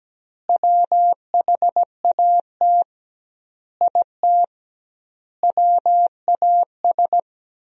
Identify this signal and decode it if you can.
Morse code, heard as a keyed tone, about 17 wpm, 700 Hz. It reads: WHAT IT WAS
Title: 2x Word-Spacing - Sentences from Top 400 Words - 17wpm